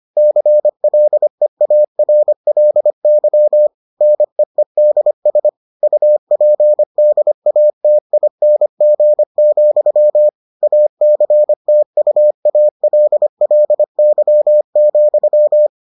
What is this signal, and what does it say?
Morse code, heard as a keyed tone, about 25 wpm, 600 Hz. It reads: CLEARLY NEEDS UPDATING, ACTUALLY,